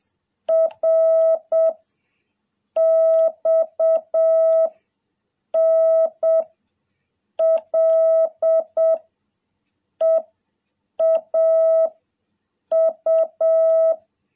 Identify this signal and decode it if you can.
Morse code, heard as a keyed tone, about 7 wpm, 650 Hz. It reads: R X N L E A U
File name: jh8cbh / 2021cw